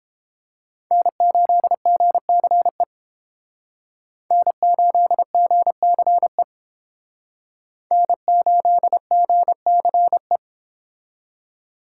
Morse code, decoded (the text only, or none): N8GCE N8GCE N8GCE